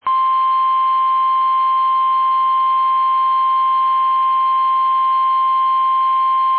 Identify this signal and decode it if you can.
Emergency Alert System (SAME) tones (the 1050 Hz attention tone). It is heard from 0.4 to 5.9 s.
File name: eas / KUSO / KUSO